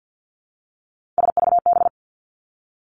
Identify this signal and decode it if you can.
Morse code, heard as a keyed tone, about 50 wpm, 700 Hz. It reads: SVB
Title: Three Challenging Letter-Number Combinations 50wpm